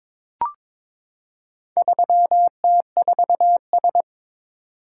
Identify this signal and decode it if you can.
Morse code, heard as a keyed tone, about 22 wpm, 700 Hz. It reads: 3T4S
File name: practice / 22wpm / Four Letter-Number Combinations - 22wpm